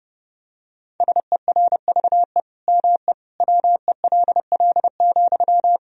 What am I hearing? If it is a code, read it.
Morse code, heard as a keyed tone, about 30 wpm, 700 Hz. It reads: SERVE ME WELL,
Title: Sentences from Top 1000 Words - Review Part 2 - 30wpm